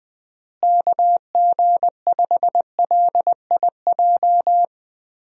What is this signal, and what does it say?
Morse code, heard as a keyed tone, about 20 wpm, 700 Hz. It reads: KG5LIJ
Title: USA Call Signs - 6 Characters - Repeated 3x - 20wpm